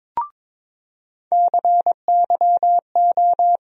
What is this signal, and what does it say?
Morse code, heard as a keyed tone, about 23 wpm, 700 Hz. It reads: CYO
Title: USA Call Sign Postfixes - 3 Letters 22wpm